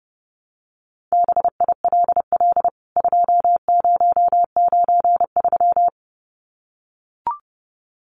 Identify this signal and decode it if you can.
Morse code, heard as a keyed tone, about 30 wpm, 700 Hz. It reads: BILL 2093
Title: CWT Exchange 30wpm